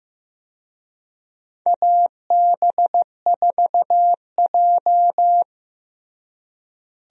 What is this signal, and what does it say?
Morse code, heard as a keyed tone, about 15 wpm, 700 Hz. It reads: AB4J
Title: USA Call Signs - 4 Characters - Repeated 3x - 15wpm